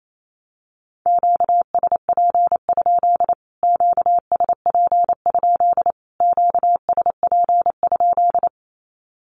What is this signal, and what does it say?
Morse code, heard as a keyed tone, about 28 wpm, 700 Hz. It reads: QSP? QSP? QSP?
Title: Q-Codes Repeated 3x 28wpm